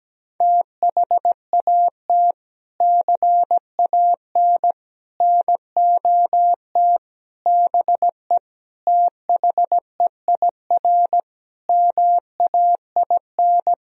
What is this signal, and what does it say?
Morse code, heard as a keyed tone, about 17 wpm, 700 Hz. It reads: THAT CAN NOT BE THEIR MAIN
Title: Sentences from Top 1000 Words - Review Part 2 - 17wpm